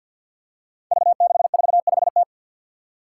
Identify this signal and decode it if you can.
Morse code, heard as a keyed tone, about 48 wpm, 700 Hz. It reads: V645T
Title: Five Challenging Letter-Number Combinations 50wpm